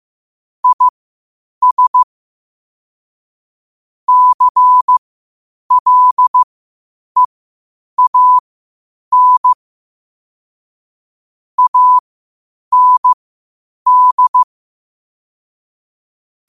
Morse code, heard as a keyed tone, about 15 wpm, spaced as for 8 wpm, 1000 Hz. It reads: IS CLEAN AND
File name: qotdfiles / 8wpm